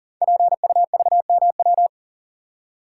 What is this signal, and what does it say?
Morse code, heard as a keyed tone, about 40 wpm, 700 Hz. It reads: PUVMW
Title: Code Groups - 5 Letters - 40wpm